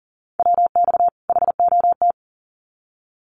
Morse code, heard as a keyed tone, about 40 wpm, 700 Hz. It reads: WX HOT
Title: Common QSO Phrases 40wpm